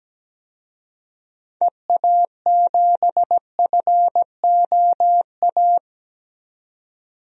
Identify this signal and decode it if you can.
Morse code, heard as a keyed tone, about 17 wpm, 700 Hz. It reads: EA7FOA
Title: DXCC-100 Call Signs - 6 Characters - 17wpm